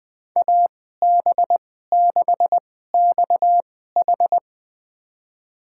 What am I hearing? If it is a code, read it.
Morse code, heard as a keyed tone, about 20 wpm, spaced as for 14 wpm, 700 Hz. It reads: AB6XH
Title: USA Call Signs - 5 Characters - 2x Character Spacing - 20wpm